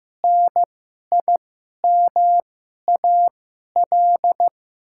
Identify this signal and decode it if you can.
Morse code, heard as a keyed tone, about 15 wpm, spaced as for 11 wpm, 700 Hz. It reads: NIMAL
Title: 2x Character Spacing - Top 200 Words - 15wpm